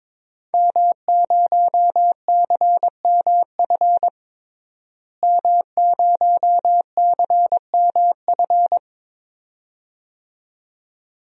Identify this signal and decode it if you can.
Morse code, heard as a keyed tone, about 22 wpm, 700 Hz. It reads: M0CMF M0CMF
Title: DXCC-100 Call Signs - 5 Characters - Repeated 2x - 22wpm